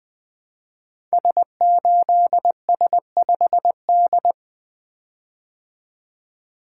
Morse code, heard as a keyed tone, about 20 wpm, 700 Hz. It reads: S8S5D
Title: Code Groups - 5 Letter:Numbers - 20wpm